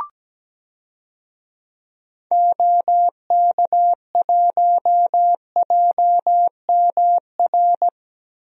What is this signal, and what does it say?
Morse code, heard as a keyed tone, about 17 wpm, 700 Hz. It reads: OK1JMR
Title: DXCC-100 Call Signs - 6 Characters - 17wpm